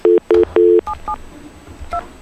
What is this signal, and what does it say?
DTMF, heard as telephone touch tones, tones of about 100 ms. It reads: 772